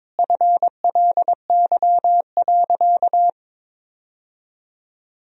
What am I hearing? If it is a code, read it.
Morse code, heard as a keyed tone, about 21 wpm, 700 Hz. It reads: FLY.
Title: Sentences from Top 1000 Words - Review Part 2 - 22wpm